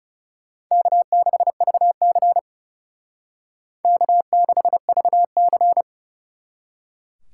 Morse code, heard as a keyed tone, about 36 wpm, 700 Hz. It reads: K6VC K6VC
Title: Speed Racing - USA 1x2 Call Signs 30wpm